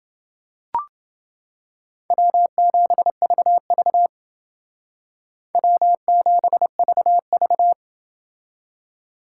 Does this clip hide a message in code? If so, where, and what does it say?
Yes: Morse code from 2.1 to 7.7 s, heard as a keyed tone, about 28 wpm, 700 Hz. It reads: W7VV W7VV